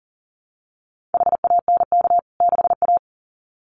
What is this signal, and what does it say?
Morse code, heard as a keyed tone, about 40 wpm, 700 Hz. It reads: HANK 6A
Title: CWT Exchange 40wpm